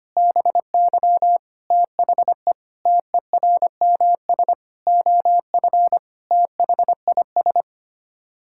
Morse code, heard as a keyed tone, about 25 wpm, 700 Hz. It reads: BY THE TERMS OF THIS
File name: Sets of 9 Words - 25wpm